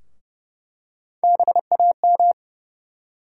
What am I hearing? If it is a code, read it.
Morse code, heard as a keyed tone, about 30 wpm, 700 Hz. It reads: BAM
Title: USA Call Sign Postfixes - 3 Letters 30wpm